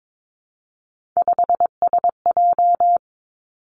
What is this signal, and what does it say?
Morse code, heard as a keyed tone, about 22 wpm, 700 Hz. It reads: 5SJ
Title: Three Challenging Letter-Number Combinations 22wpm